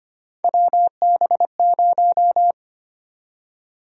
Morse code, heard as a keyed tone, about 25 wpm, 700 Hz. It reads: WB0
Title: USA Call Sign Prefixes - 25wpm